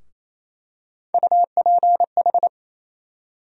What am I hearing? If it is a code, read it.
Morse code, heard as a keyed tone, about 28 wpm, 700 Hz. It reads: UPH
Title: USA Call Sign Postfixes - 3 Letters 28wpm